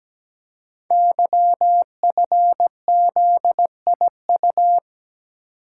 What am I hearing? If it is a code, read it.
Morse code, heard as a keyed tone, about 17 wpm, 700 Hz. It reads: YFZIU
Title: Code Groups - 5 Letters - 17wpm